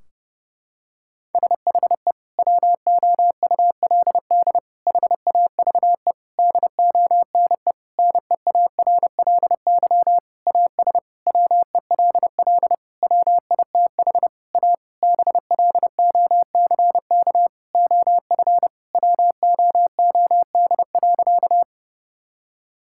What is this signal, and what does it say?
Morse code, heard as a keyed tone, about 32 wpm, 700 Hz. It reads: SHE WOULD HAVE DONE NEARLY AS WELL WITH A BLOCK OF WOOD.
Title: Sentences from Top 1600 Words 30wpm